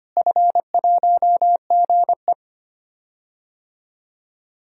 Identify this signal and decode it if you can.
Morse code, heard as a keyed tone, about 25 wpm, 700 Hz. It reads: F1GE
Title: DXCC-100 Call Signs - 4 Characters - Repeated 3x - 25wpm